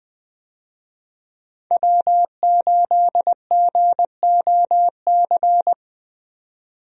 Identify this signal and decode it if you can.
Morse code, heard as a keyed tone, about 20 wpm, 700 Hz. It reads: W8GOC
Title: USA Call Signs - 5 Characters - 20wpm